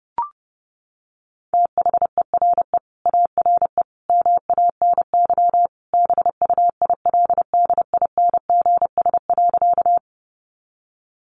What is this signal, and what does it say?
Morse code, heard as a keyed tone, about 30 wpm, 700 Hz. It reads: THERE ARE MANY BUILDINGS.
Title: Easy Phrases 30wpm